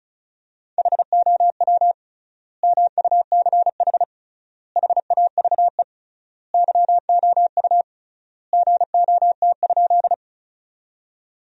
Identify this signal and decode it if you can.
Morse code, heard as a keyed tone, about 36 wpm, 700 Hz. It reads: HOW MUCH HAVE YOU GOT?
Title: 3x Word-Spacing - Sentences from Top 300 Words - 35wpm